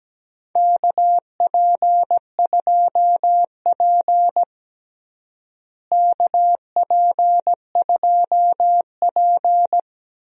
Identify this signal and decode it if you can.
Morse code, heard as a keyed tone, about 17 wpm, 700 Hz. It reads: KP2P KP2P
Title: DXCC-100 Call Signs - 4 Characters - Repeated 2x - 17wpm